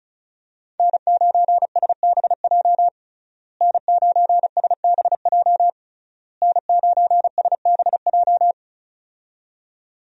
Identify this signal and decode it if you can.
Morse code, heard as a keyed tone, about 34 wpm, 700 Hz. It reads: N9SBJ N9SBJ N9SBJ
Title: USA Call Signs - 5 Characters - Repeated 3x - 35wpm